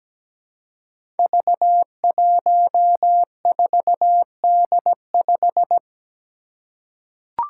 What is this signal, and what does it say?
Morse code, heard as a keyed tone, about 17 wpm, 700 Hz. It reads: V14D5
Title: Five Challenging Letter-Number Combinations 17wpm